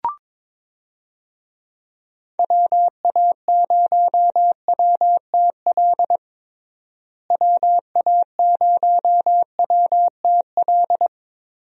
Morse code, heard as a keyed tone, about 22 wpm, 700 Hz. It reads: WA0WTL WA0WTL
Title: USA Call Signs - 6 Characters - Repeated 2x - 22wpm